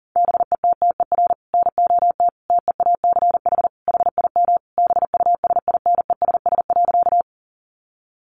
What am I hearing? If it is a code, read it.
Morse code, heard as a keyed tone, about 37 wpm, 700 Hz. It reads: BETTER NOT TEACH HIM BUSINESS.
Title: Sentences from Top 600 Words 40wpm